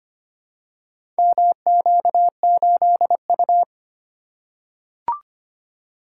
Morse code, heard as a keyed tone, about 25 wpm, 700 Hz. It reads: MQ8U
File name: practice / 25wpm / Four Letter-Number Combinations - 25wpm